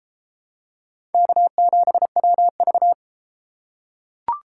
Morse code, heard as a keyed tone, about 31 wpm, 700 Hz. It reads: K7WV